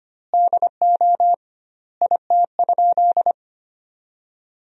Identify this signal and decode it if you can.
Morse code, heard as a keyed tone, about 25 wpm, 700 Hz. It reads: DO IT?
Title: 2x Word-Spacing - Sentences from Top 100 Words - 25wpm